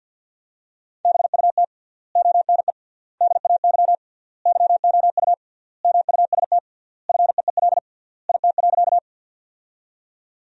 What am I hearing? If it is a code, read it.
Morse code, heard as a keyed tone, about 50 wpm, 700 Hz. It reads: BUT ONE DAY YOU MUST FEEL IT.
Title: 3x Word-Spacing - Sentences from Top 400 Words - 50wpm